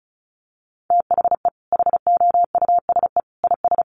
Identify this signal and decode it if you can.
Morse code, heard as a keyed tone, about 34 wpm, 700 Hz. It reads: THE HOUSE IS
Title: Sentences from Top 1200 Words 35wpm